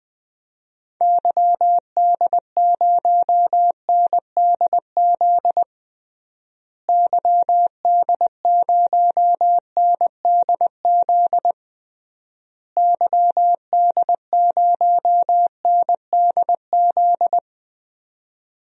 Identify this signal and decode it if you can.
Morse code, heard as a keyed tone, about 21 wpm, 700 Hz. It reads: YD0NDZ YD0NDZ YD0NDZ
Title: DXCC-100 Call Signs - 6 Characters - Repeated 3x - 20wpm